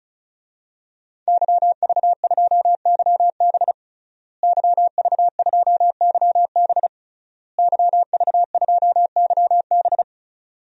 Morse code, heard as a keyed tone, about 36 wpm, 700 Hz. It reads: YV2YB YV2YB YV2YB
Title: DXCC-100 Call Signs - 5 Characters - Repeated 3x - 35wpm